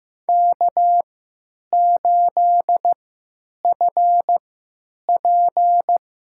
Morse code, heard as a keyed tone, about 15 wpm, spaced as for 9 wpm, 700 Hz. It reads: K8FP
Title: USA Call Signs - 4 Characters - 3x Character Spacing - 15wpm